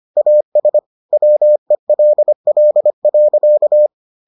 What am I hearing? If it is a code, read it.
Morse code, heard as a keyed tone, about 25 wpm, 600 Hz. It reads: AS WELL.